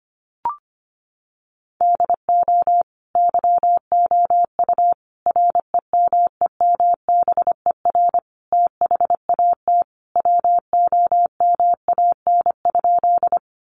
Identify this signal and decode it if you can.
Morse code, heard as a keyed tone, about 25 wpm, 700 Hz. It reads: DO YOU REMEMBER THAT WOMAN?
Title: Sentences from Top 1100 Words 25wpm